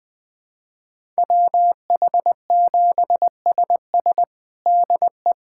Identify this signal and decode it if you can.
Morse code, heard as a keyed tone, about 20 wpm, 700 Hz. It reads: WH7SS DE
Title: USA Callsign DE Callsign - 5 Characters - 20wpm